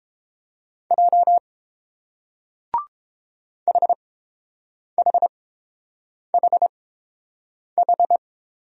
Morse code, heard as a keyed tone, about 30 wpm, spaced as for 7 wpm, 700 Hz. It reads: J HHHH